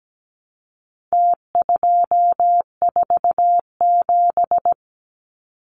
Morse code, heard as a keyed tone, about 17 wpm, 700 Hz. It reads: T247
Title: Numbers - Contest Style - 4-Digit 17wpm